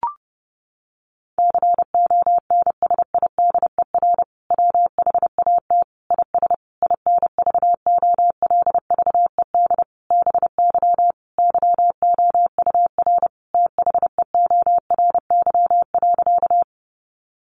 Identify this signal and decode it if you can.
Morse code, heard as a keyed tone, about 30 wpm, 700 Hz. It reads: CONSIDER WHAT IS INVOLVED BY YOUR THEORY.
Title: Sentences from Top 800 Words 30wpm